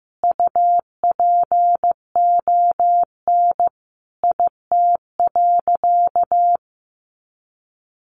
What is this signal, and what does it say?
Morse code, heard as a keyed tone, about 15 wpm, 700 Hz. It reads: UPON IT.
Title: Sentences from Top 1500 Words 15wpm